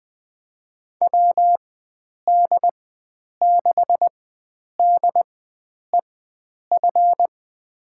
Morse code, heard as a keyed tone, about 21 wpm, spaced as for 9 wpm, 700 Hz. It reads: WD6DEF
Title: USA Call Signs - 6 Characters - 4x Character Spacing - 20wpm